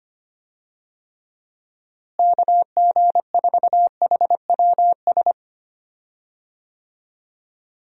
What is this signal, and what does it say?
Morse code, heard as a keyed tone, about 25 wpm, 700 Hz. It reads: KG4HWS
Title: USA Call Signs - 6 Characters - Repeated 2x - 25wpm